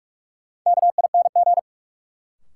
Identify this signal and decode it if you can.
Morse code, heard as a keyed tone, about 46 wpm, 700 Hz. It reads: KING
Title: Sentences from Top 400 Words - New Word Drill - 45wpm